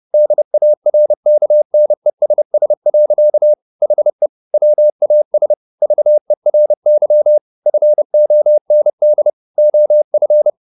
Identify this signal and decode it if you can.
Morse code, heard as a keyed tone, about 30 wpm, 600 Hz. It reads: DARKNESS. HE WAS VERY FOND OF
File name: the old man and the sea / the old man and the sea 1 30wpm